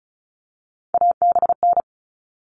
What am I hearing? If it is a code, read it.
Morse code, heard as a keyed tone, about 34 wpm, 700 Hz. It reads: ABN
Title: All Cut Numbers - 3-Digit 35wpm